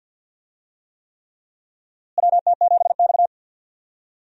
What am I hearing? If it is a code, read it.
Morse code, heard as a keyed tone, about 52 wpm, 700 Hz. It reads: WT7X
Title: USA Call Signs - 4 Characters - Repeated 3x - 50wpm